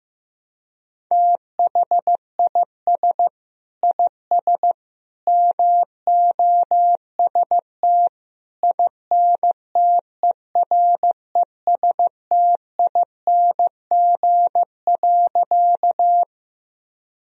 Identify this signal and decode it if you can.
Morse code, heard as a keyed tone, about 15 wpm, 700 Hz. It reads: THIS IS MOST INTERESTING.
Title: Sentences from Top 1000 Words - Review Part 1 - 15wpm